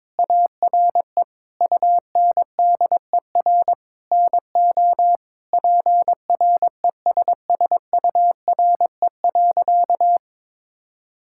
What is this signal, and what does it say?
Morse code, heard as a keyed tone, about 22 wpm, 700 Hz. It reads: ARE UNDER NO PRESSURE.
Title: Sentences from Top 1000 Words - Review Part 6 - 22wpm